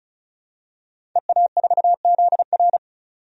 Morse code, heard as a keyed tone, about 36 wpm, 700 Hz. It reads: EA4ZR
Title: DXCC-100 Call Signs - 5 Characters - 35wpm